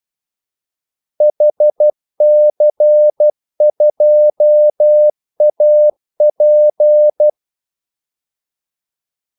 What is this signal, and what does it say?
Morse code, heard as a keyed tone, about 12 wpm, 600 Hz. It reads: HC2AP
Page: 2 ekstra ordmellemrum mellem kaldesignalerne